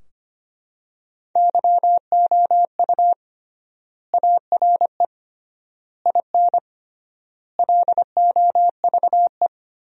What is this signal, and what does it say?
Morse code, heard as a keyed tone, about 25 wpm, 700 Hz. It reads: YOU ARE IN LOVE
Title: 3x Word-Spacing - Sentences from Top 400 Words - 25wpm